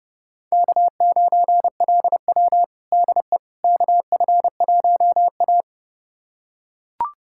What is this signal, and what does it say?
Morse code, heard as a keyed tone, about 30 wpm, 700 Hz. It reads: K9LW DE KF1A
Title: USA Callsign DE Callsign - 4 Characters - 30wpm